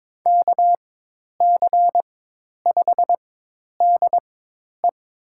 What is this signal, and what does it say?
Morse code, heard as a keyed tone, about 21 wpm, 700 Hz. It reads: K C 5 D E
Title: USA Call Signs - 6 Characters - 4x Character Spacing - 22wpm